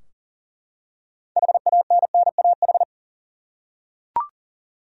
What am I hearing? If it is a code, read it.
Morse code, heard as a keyed tone, about 40 wpm, 700 Hz. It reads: HANNAH
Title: USA Common Names - 40wpm